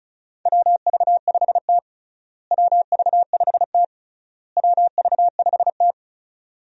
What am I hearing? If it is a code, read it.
Morse code, heard as a keyed tone, about 36 wpm, 700 Hz. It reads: WV5T WV5T WV5T
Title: USA Call Signs - 4 Characters - Repeated 3x - 35wpm